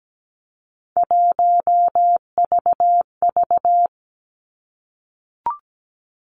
Morse code, heard as a keyed tone, about 17 wpm, 700 Hz. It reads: 1VV
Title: Three Challenging Letter-Number Combinations 17wpm